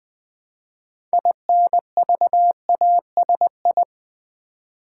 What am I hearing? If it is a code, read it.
Morse code, heard as a keyed tone, about 20 wpm, 700 Hz. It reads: INVASI
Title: Build A Word - 3 Letters At A Time - 8-Letter Words - 20wpm